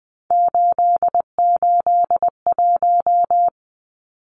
Morse code, heard as a keyed tone, about 20 wpm, 700 Hz. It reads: 881
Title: Numbers - 3-Digit 20wpm